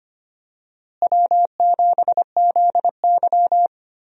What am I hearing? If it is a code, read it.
Morse code, heard as a keyed tone, about 25 wpm, 700 Hz. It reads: W7ZY
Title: USA Call Signs - 4 Characters - Repeated 3x - 25wpm